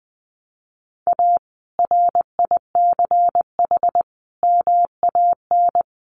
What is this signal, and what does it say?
Morse code, heard as a keyed tone, about 20 wpm, 700 Hz. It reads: A RICH MAN
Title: Sentences from Top 1100 Words 20wpm